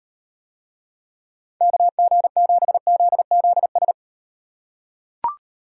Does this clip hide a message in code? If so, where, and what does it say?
Yes: Morse code from 1.6 to 3.9 s, heard as a keyed tone, about 37 wpm, 700 Hz. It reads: KG7ZZS